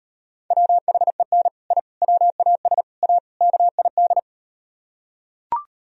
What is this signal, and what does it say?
Morse code, heard as a keyed tone, about 38 wpm, 700 Hz. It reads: WHEN I WAS A KID